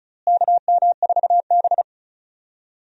Morse code, heard as a keyed tone, about 34 wpm, 700 Hz. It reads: KM4B
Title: USA Call Signs - 4 Characters - Repeated 3x - 35wpm